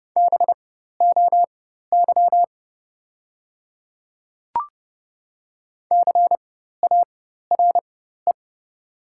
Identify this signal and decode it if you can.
Morse code, heard as a keyed tone, about 30 wpm, spaced as for 14 wpm, 700 Hz. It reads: BOY CARE
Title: 4x Character Spacing - Top 300 Words - 30wpm